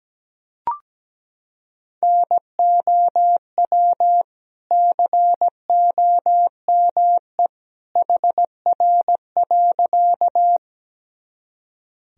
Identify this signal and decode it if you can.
Morse code, heard as a keyed tone, about 17 wpm, 700 Hz. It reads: NOW COME HR.
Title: Sentences from Top 200 Words and Abbreviations 17wpm